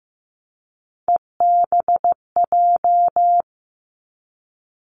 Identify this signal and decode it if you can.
Morse code, heard as a keyed tone, about 15 wpm, 700 Hz. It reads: EBJ
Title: Three Challenging Letter-Number Combinations 15wpm